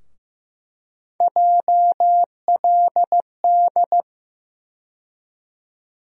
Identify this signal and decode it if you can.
Morse code, heard as a keyed tone, about 15 wpm, 700 Hz. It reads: JLD
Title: USA Call Sign Postfixes - 3 Letters 15wpm